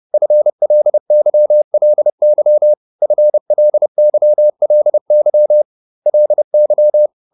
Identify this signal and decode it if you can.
Morse code, heard as a keyed tone, about 30 wpm, 600 Hz. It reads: FLYLY FLYLY LY